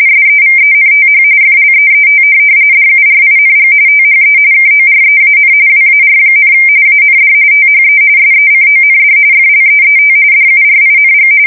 RTTY, about 45.45 baud. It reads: SEVENTH FLEET DURING THE KOREAN WAR, AND THE SECOND FLEET DURING
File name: fsk170 / USS-New-Jersey